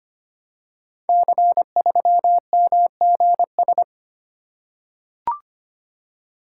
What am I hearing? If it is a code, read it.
Morse code, heard as a keyed tone, about 25 wpm, 700 Hz. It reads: C3MGS
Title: Code Groups - 5 Letter:Numbers - 25wpm